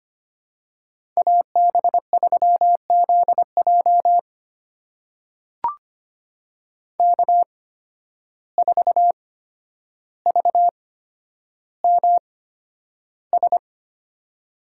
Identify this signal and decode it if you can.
Morse code, heard as a keyed tone, about 25 wpm, spaced as for 11 wpm, 700 Hz. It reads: AB3ZJ K4VMS
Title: USA Call Signs - 5 Characters - 8x Character Spacing - 25wpm